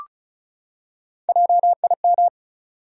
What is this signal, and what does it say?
Morse code, heard as a keyed tone, about 36 wpm, 700 Hz. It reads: JIM
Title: USA Common Names - Max 4 Letters - 35wpm